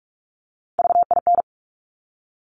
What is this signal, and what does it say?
Morse code, heard as a keyed tone, about 46 wpm, 700 Hz. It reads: VIN
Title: Top Three Letter Combinations in Words 45wpm